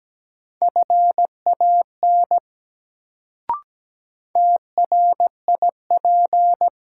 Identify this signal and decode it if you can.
Morse code, heard as a keyed tone, about 17 wpm, 700 Hz. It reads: FAN TRIP
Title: Sentences from Top 1000 Words - Review Part 4 - New Word Drill - 17wpm